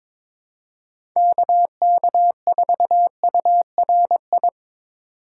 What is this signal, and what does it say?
Morse code, heard as a keyed tone, about 23 wpm, 700 Hz. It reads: KK4URI
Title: USA Call Signs - 4x Character Spacing - 22wpm